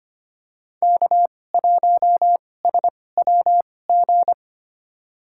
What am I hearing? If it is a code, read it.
Morse code, heard as a keyed tone, about 25 wpm, spaced as for 18 wpm, 700 Hz. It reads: K1SWG
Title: USA Call Signs - 5 Characters - 2x Character Spacing - 25wpm